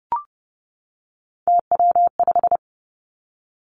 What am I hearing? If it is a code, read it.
Morse code, heard as a keyed tone, about 32 wpm, 700 Hz. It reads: TW5
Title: Three Challenging Letter-Number Combinations 30wpm